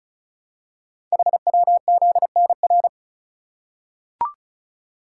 Morse code, heard as a keyed tone, about 36 wpm, 700 Hz. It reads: HWZNR
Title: Code Groups - 5 Letter:Numbers - 35wpm